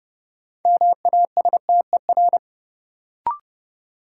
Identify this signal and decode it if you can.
Morse code, heard as a keyed tone, about 30 wpm, 700 Hz. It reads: MASTER
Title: Sentences from Top 1300 Words - New Word Drill - 30wpm